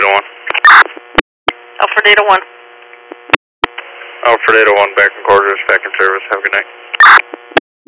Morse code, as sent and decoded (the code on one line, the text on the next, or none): none